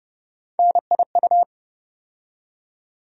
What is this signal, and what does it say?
Morse code, heard as a keyed tone, about 30 wpm, 700 Hz. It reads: NIU
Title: USA Call Sign Postfixes - 3 Letters 30wpm